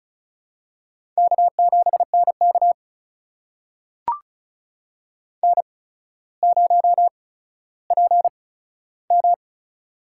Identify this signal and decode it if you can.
Morse code, heard as a keyed tone, about 36 wpm, spaced as for 15 wpm, 700 Hz. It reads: K7NK N0PM
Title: USA Call Signs - 4 Characters - 8x Character Spacing - 35wpm